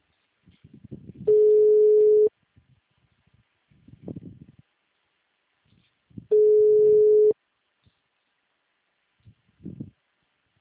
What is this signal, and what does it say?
Ringback tone (Europe, 425 Hz).